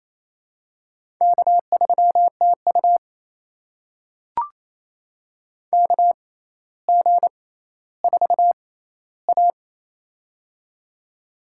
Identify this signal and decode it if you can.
Morse code, heard as a keyed tone, about 28 wpm, spaced as for 14 wpm, 700 Hz. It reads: K3TU KG4A
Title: USA Call Signs - 4 Characters - 6x Character Spacing - 28wpm